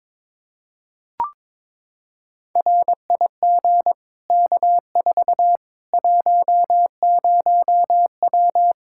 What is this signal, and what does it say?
Morse code, heard as a keyed tone, about 22 wpm, 700 Hz. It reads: RIG K4 10W
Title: Practice QSO - Exchange 2 - Info - 22wpm